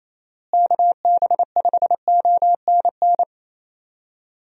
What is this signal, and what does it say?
Morse code, heard as a keyed tone, about 27 wpm, 700 Hz. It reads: KB5ONN
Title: USA Call Signs - 6 Characters - Repeated 3x - 28wpm